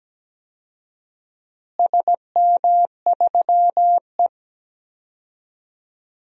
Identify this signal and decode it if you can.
Morse code, heard as a keyed tone, about 17 wpm, 700 Hz. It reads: SM3E
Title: DXCC-100 Call Signs - 4 Characters - Repeated 3x - 17wpm